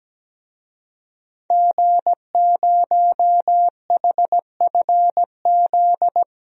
Morse code, heard as a keyed tone, about 17 wpm, 700 Hz. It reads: G0HFZ
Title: DXCC-100 Call Signs - 5 Characters - Repeated 3x - 17wpm